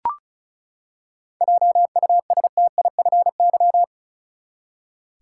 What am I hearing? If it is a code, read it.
Morse code, heard as a keyed tone, about 36 wpm, 700 Hz. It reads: JUSTIFY
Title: Seven Letter Words - 35wpm